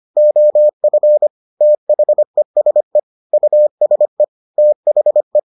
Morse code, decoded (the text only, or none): OF THESE USE THE